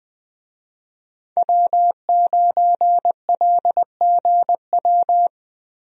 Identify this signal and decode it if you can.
Morse code, heard as a keyed tone, about 20 wpm, 700 Hz. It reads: W9LGW
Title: USA Call Signs - 5 Characters - Repeated 2x - 20wpm